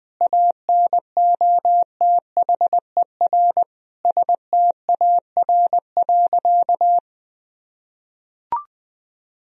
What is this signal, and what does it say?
Morse code, heard as a keyed tone, about 21 wpm, 700 Hz. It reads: ANOTHER STAR.
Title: Sentences from Top 1000 Words - Review Part 2 - 20wpm